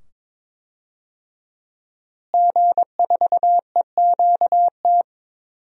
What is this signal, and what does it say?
Morse code, heard as a keyed tone, about 22 wpm, 700 Hz. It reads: G4EQT
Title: DXCC-100 Call Signs - 5 Characters - Repeated 2x - 22wpm